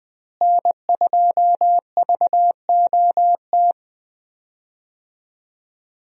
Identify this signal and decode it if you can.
Morse code, heard as a keyed tone, about 20 wpm, 700 Hz. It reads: N2VOT
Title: USA Call Signs - 5 Characters - Repeated 3x - 20wpm